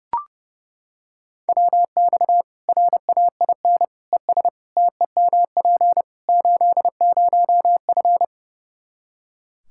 Morse code, heard as a keyed tone, about 30 wpm, 700 Hz. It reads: WX RAIN ES TEMP 80F
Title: Practice QSO - Exchange 2 - Info - 30wpm